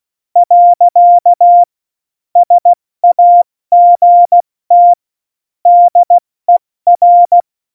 Morse code, heard as a keyed tone, about 16 wpm, 700 Hz. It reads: . SAGT DER
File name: Morsekurs / Witze / Witz 17.Juli 2018, 70 F80